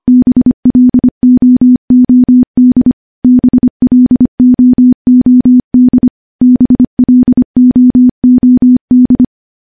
Morse code, heard as a keyed tone, about 25 wpm, 250 Hz. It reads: BLOOD BLOOD BLOOD
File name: morse-training-25WPM